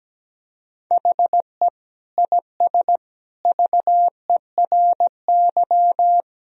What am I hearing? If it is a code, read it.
Morse code, heard as a keyed tone, about 17 wpm, 700 Hz. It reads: HE IS VERY
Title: Sentences from Top 1000 Words - Review Part 1 - 17wpm